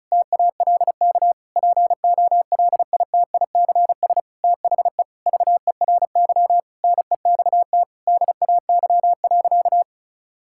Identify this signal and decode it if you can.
Morse code, heard as a keyed tone, about 36 wpm, 700 Hz. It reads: TALK POLITICS THE VERY NEXT DAY.
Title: Sentences from Top 1000 Words - Review Part 6 - 35wpm